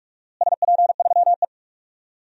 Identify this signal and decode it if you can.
Morse code, heard as a keyed tone, about 42 wpm, 700 Hz. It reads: SP3E
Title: DXCC-100 Call Signs - 4 Characters - Repeated 2x - 45wpm